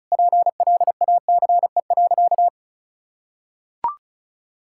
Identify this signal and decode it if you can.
Morse code, heard as a keyed tone, about 34 wpm, 700 Hz. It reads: PLACE.
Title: Sentences from Top 1600 Words 35wpm